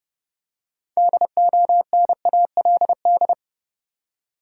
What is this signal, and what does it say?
Morse code, heard as a keyed tone, about 31 wpm, 700 Hz. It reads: DONALD
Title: USA Common Names - 30wpm